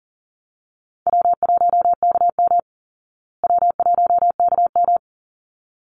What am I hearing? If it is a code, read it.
Morse code, heard as a keyed tone, about 40 wpm, 700 Hz. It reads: W1KM W1KM
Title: Easy Call Signs Repeated 2x - 40wpm